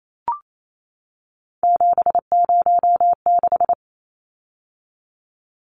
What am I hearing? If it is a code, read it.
Morse code, heard as a keyed tone, about 28 wpm, 700 Hz. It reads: 706
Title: Numbers - 3-Digit 28wpm